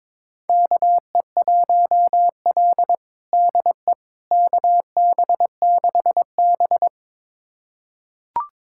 Morse code, heard as a keyed tone, about 22 wpm, 700 Hz. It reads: KE1L DE KB6B
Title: USA Callsign DE Callsign - 4 Characters - 22wpm